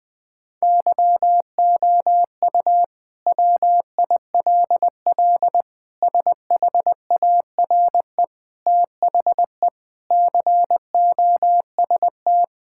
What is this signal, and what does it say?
Morse code, heard as a keyed tone, about 21 wpm, 700 Hz. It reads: YOU WILL SHARE THE COST
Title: Sentences from Top 1000 Words - Review Part 6 - 20wpm